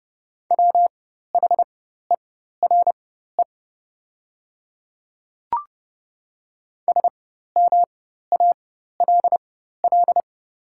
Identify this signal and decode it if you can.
Morse code, heard as a keyed tone, about 30 wpm, spaced as for 14 wpm, 700 Hz. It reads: WHERE SMALL
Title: 4x Character Spacing - Top 300 Words - 30wpm